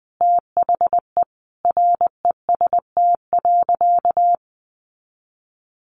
Morse code, heard as a keyed tone, about 20 wpm, 700 Hz. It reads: THE REST.
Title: Sentences from Top 1400 Words 20wpm